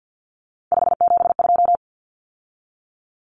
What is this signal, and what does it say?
Morse code, heard as a keyed tone, about 50 wpm, 700 Hz. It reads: 572
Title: Common Cut Numbers - 3-Digit 50wpm